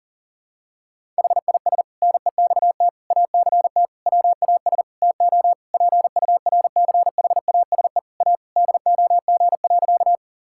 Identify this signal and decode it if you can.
Morse code, heard as a keyed tone, about 37 wpm, 700 Hz. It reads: HIS NEXT ACT WAS TO PURCHASE A DOG.
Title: Sentences from Top 1000 Words - Review Part 5 - 40wpm